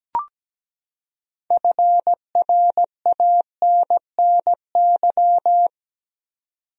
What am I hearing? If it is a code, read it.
Morse code, heard as a keyed tone, about 17 wpm, 700 Hz. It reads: FRANNY
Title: USA Common Names - 17wpm